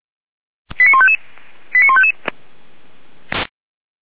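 Five-tone selective call (ZVEI), heard as quick runs of five tones, tones of about 70 ms. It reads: 89155 89155
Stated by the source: Received via SDR, ZVEI five-tone call.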